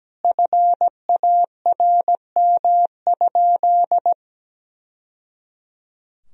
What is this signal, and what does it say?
Morse code, heard as a keyed tone, about 17 wpm, 700 Hz. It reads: FARM?
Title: Sentences from Top 1000 Words - Review Part 3 - 17wpm